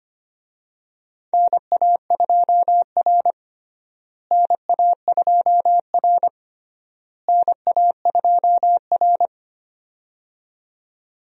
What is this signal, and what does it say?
Morse code, heard as a keyed tone, about 25 wpm, 700 Hz. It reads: NA2R NA2R NA2R
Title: USA Call Signs - 4 Characters - Repeated 3x - 25wpm